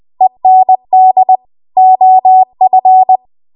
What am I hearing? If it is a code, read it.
Morse code, heard as a keyed tone, about 20 wpm, 750 Hz. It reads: END OF
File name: Archive / 20 WPM / 20WPM